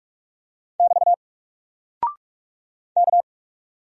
Morse code, heard as a keyed tone, about 44 wpm, 700 Hz. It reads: <BT> K
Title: Single Character (Letter:Number:Punctuation:Prosign) 45wpm